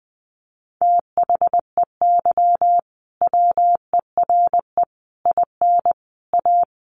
Morse code, heard as a keyed tone, about 19 wpm, 700 Hz. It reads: THEY WERE IN A
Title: Sentences from Top 1400 Words 20wpm